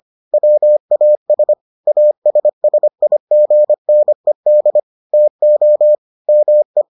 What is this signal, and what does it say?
Morse code, heard as a keyed tone, about 25 wpm, 600 Hz. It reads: WAS ASSIGNED TO ME